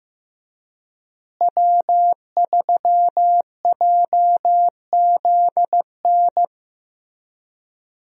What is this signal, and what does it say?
Morse code, heard as a keyed tone, about 16 wpm, 700 Hz. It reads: W3JZN
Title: USA Call Signs - 5 Characters - Repeated 2x - 15wpm